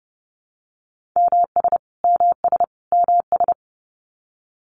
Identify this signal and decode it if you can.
Morse code, heard as a keyed tone, about 30 wpm, 700 Hz. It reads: MS MS MS
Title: US States Repeated 3x 30wpm